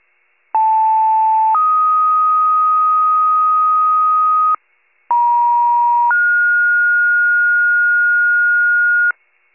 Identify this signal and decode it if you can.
Two-tone page, heard as a one-second tone, then a three-second tone, about 880 then 1250 Hz, about 950 then 1430 Hz, with noise. It is heard from 0.5 to 9.1 s.
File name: pager / 83 FIRE / wcalert